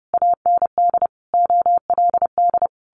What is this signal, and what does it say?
Morse code, heard as a keyed tone, about 30 wpm, 700 Hz. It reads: AND OLD
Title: A Sound Of Thunder 30wpm